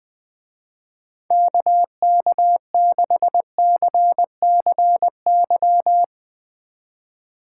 Morse code, heard as a keyed tone, about 21 wpm, 700 Hz. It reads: KK6CCY
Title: Speed Racing - USA 2x3 Call Signs 20wpm